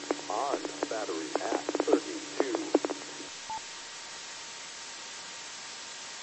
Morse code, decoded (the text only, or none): none